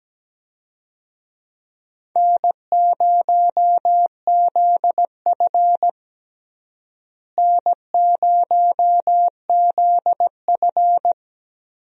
Morse code, heard as a keyed tone, about 17 wpm, 700 Hz. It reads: N0ZF N0ZF
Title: USA Call Signs - 4 Characters - Repeated 3x - 17wpm